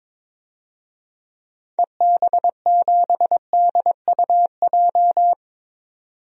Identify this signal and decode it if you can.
Morse code, heard as a keyed tone, about 22 wpm, 700 Hz. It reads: EB7DUJ
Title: DXCC-100 Call Signs - 6 Characters - 22wpm